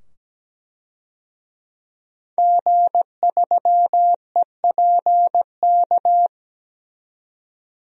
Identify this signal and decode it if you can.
Morse code, heard as a keyed tone, about 17 wpm, 700 Hz. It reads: G3EPK
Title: DXCC-100 Call Signs - 5 Characters - 17wpm